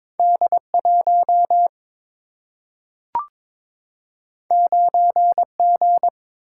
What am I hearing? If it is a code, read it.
Morse code, heard as a keyed tone, about 22 wpm, 700 Hz. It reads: D1 9G
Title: Single-Letter Single-Number 22wpm